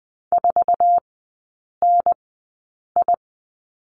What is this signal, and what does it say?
Morse code, heard as a keyed tone, about 20 wpm, spaced as for 8 wpm, 700 Hz. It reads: 4NI
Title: 081 Lesson 14e - Number 2 - Three Characters 20wpm